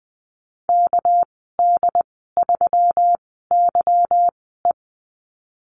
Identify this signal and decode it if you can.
Morse code, heard as a keyed tone, about 20 wpm, spaced as for 14 wpm, 700 Hz. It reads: KD3YE
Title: 238 Lesson 38g - Letter q - Call Signs 20wpm